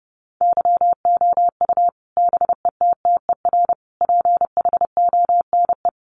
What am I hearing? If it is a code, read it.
Morse code, heard as a keyed tone, about 30 wpm, 700 Hz. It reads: YOU BETTER PHONE
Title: Sentences from Top 1100 Words 30wpm